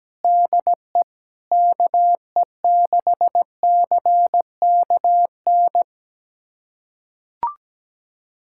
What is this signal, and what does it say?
Morse code, heard as a keyed tone, about 17 wpm, 700 Hz. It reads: DE KE6CKN
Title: USA Callsign DE Callsign 17wpm